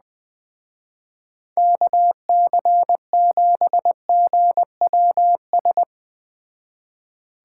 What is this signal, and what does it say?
Morse code, heard as a keyed tone, about 20 wpm, 700 Hz. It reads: KC7GWS
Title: Speed Racing - USA 2x3 Call Signs 17wpm